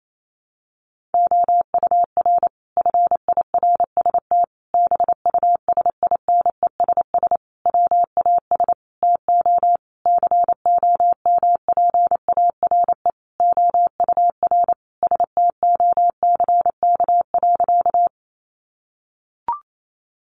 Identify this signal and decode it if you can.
Morse code, heard as a keyed tone, about 28 wpm, 700 Hz. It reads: OUR FIRST BUSINESS WAS TO COMPARE OUR STOCK.
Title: Sentences from Top 1300 Words 28wpm